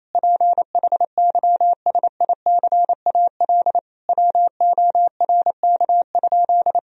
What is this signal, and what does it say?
Morse code, heard as a keyed tone, about 28 wpm, 700 Hz. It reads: PHYSICAL WORK?
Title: Sentences from Top 1000 Words - Review Part 4 - 28wpm